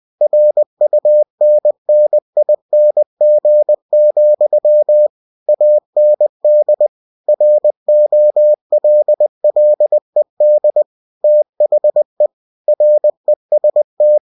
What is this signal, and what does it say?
Morse code, heard as a keyed tone, about 20 wpm, 600 Hz. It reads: RUNNING, AND ROLLED THE REST